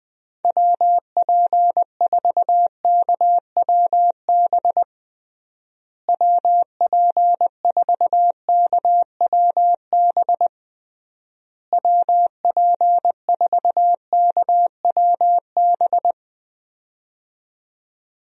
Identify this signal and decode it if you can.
Morse code, heard as a keyed tone, about 20 wpm, 700 Hz. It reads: WP4KWB WP4KWB WP4KWB
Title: USA Call Signs - 6 Characters - Repeated 3x - 20wpm